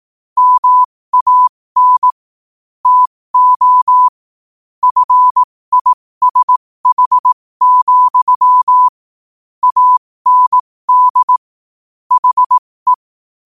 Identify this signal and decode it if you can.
Morse code, heard as a keyed tone, about 18 wpm, spaced as for 15 wpm, 1000 Hz. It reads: MAN TO FISH, AND HE
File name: qotdfiles / 15wpm